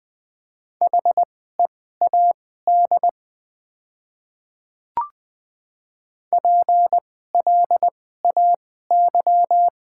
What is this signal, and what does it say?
Morse code, heard as a keyed tone, about 20 wpm, spaced as for 15 wpm, 700 Hz. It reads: HEAD PLAY
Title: 2x Character Spacing - Top 300 Words - 20wpm